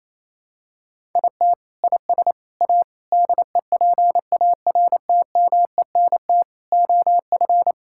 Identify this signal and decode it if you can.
Morse code, heard as a keyed tone, about 27 wpm, 700 Hz. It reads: IT IS A DEPARTMENT OF
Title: Sentences from Top 1000 Words - Review Part 5 - 28wpm